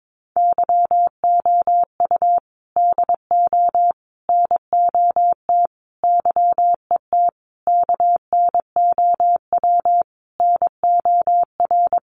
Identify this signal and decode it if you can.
Morse code, heard as a keyed tone, about 22 wpm, 700 Hz. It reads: YOU DO NOT YET KNOW NOR
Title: Sentences from Top 1500 Words 22wpm